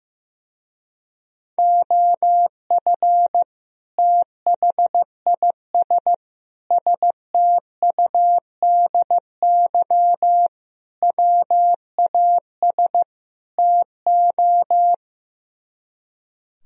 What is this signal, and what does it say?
Morse code, heard as a keyed tone, about 15 wpm, 700 Hz. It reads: OF THIS STUDY WAS TO
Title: Speed-Racing - Sets of 5 Words 15wpm